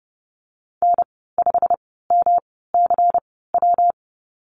Morse code, heard as a keyed tone, about 30 wpm, spaced as for 17 wpm, 700 Hz. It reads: N5MCW
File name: Lesson 19g - Letter w - Call Signs 30wpm